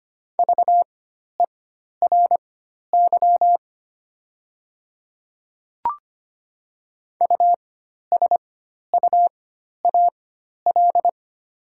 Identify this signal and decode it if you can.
Morse code, heard as a keyed tone, about 25 wpm, spaced as for 12 wpm, 700 Hz. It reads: VERY USUAL
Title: 4x Character Spacing - Top 500 Words - 25wpm